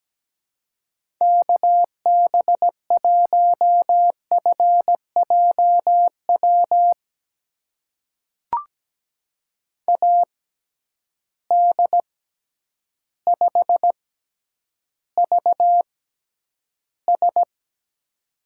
Morse code, heard as a keyed tone, about 17 wpm, spaced as for 9 wpm, 700 Hz. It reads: KB1FJW AD5VS
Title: USA Call Signs - 6x Character Spacing - 17wpm